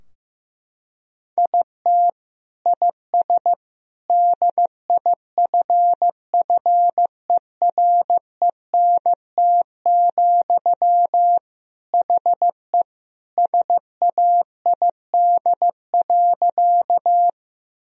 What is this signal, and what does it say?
Morse code, heard as a keyed tone, about 16 wpm, 700 Hz. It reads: IT IS DIFFERENT, HE SAID.
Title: Sentences from Top 1000 Words - Review Part 3 - 15wpm